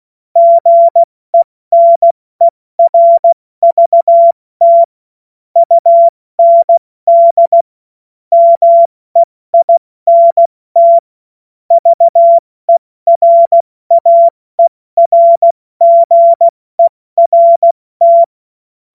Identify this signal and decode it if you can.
Morse code, heard as a keyed tone, about 16 wpm, 680 Hz. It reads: GENERVT UND MEINT VERAERGERT
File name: Morsekurs / Witze / Witz 17. Januar 2023 70 F80